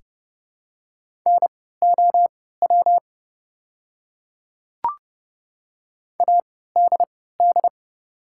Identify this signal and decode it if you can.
Morse code, heard as a keyed tone, about 30 wpm, spaced as for 17 wpm, 690 Hz. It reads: NOW ADD
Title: 3x Character Spacing - Top 200 Words - 30wpm